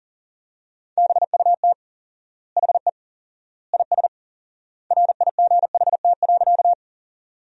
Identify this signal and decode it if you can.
Morse code, heard as a keyed tone, about 37 wpm, 700 Hz. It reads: BUT HE IS RIGHT.
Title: 4x Word-Spacing - Sentences from Top 200 Words - 40wpm